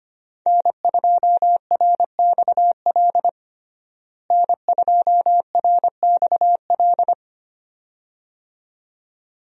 Morse code, heard as a keyed tone, about 25 wpm, 700 Hz. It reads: N2RXL N2RXL
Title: USA Call Signs - 5 Characters - Repeated 2x - 25wpm